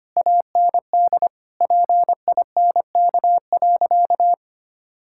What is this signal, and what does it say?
Morse code, heard as a keyed tone, about 25 wpm, 700 Hz. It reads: AND PINK.
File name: practice / 25wpm / Difficult Sentences with Hints - 25wpm